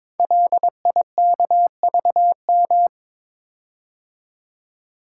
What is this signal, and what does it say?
Morse code, heard as a keyed tone, about 22 wpm, 700 Hz. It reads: LIKVM
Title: Code Groups - 5 Letters - 22wpm